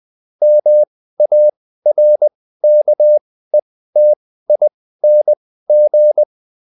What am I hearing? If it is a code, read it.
Morse code, heard as a keyed tone, about 20 wpm, spaced as for 14 wpm, 600 Hz. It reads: MARKETING